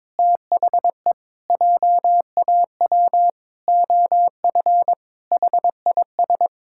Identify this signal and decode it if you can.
Morse code, heard as a keyed tone, about 22 wpm, 700 Hz. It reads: THE JAW OF HIS
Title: Pangram Challenge v2 - 22wpm